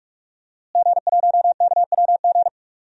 Morse code, heard as a keyed tone, about 46 wpm, 700 Hz. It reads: G1KWG
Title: DXCC-100 Call Signs - 5 Characters - Repeated 2x - 45wpm